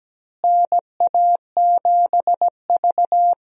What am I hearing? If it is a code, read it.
Morse code, heard as a keyed tone, about 17 wpm, 700 Hz. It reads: NA7V
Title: USA Call Signs - 4 Characters - Repeated 2x - 17wpm